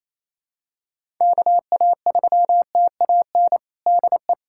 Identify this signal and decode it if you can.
Morse code, heard as a keyed tone, about 28 wpm, 700 Hz. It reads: KA3TAN DE
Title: USA Callsign DE Callsign - 6 Characters - 28wpm